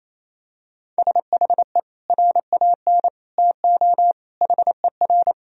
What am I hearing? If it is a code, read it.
Morse code, heard as a keyed tone, about 28 wpm, 700 Hz. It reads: SHE RAN TO HER
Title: Sentences from Top 1000 Words - Review Part 3 - 28wpm